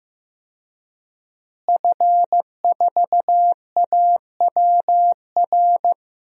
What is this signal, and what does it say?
Morse code, heard as a keyed tone, about 15 wpm, 700 Hz. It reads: F4AWR
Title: DXCC-100 Call Signs - 5 Characters - Repeated 2x - 15wpm